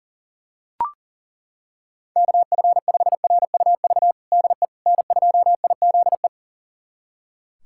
Morse code, heard as a keyed tone, about 37 wpm, 700 Hz. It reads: KF5RUV DE N2IZE
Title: USA Callsign DE Callsign 40wpm